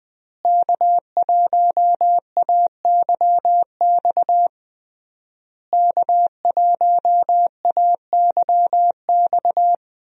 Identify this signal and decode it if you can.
Morse code, heard as a keyed tone, about 20 wpm, 700 Hz. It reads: K1AYX K1AYX
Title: USA Call Signs - 5 Characters - Repeated 3x - 20wpm